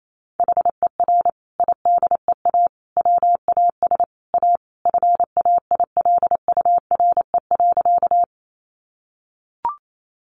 Morse code, heard as a keyed tone, about 28 wpm, 700 Hz. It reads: HER IDEA WAS A FAILURE.
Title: Sentences from Top 900 Words 28wpm